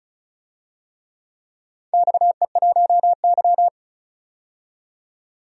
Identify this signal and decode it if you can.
Morse code, heard as a keyed tone, about 36 wpm, 700 Hz. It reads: XE1Y
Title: DXCC-100 Call Signs - 4 Characters - 35wpm